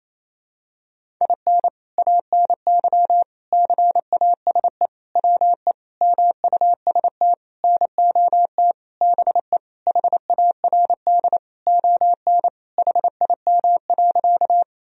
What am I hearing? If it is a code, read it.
Morse code, heard as a keyed tone, about 28 wpm, 700 Hz. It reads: IN ANY CASE WE MUST NOT BE HARD ON HIM.
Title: Sentences from Top 1000 Words - Review Part 1 - 28wpm